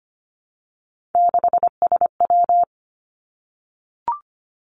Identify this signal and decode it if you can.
Morse code, heard as a keyed tone, about 25 wpm, 700 Hz. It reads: 6SW
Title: Three Challenging Letter-Number Combinations 25wpm